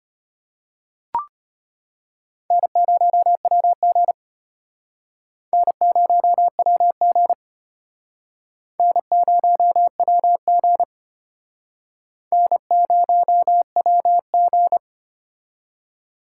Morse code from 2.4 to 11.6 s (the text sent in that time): N0WG N0WG N0WG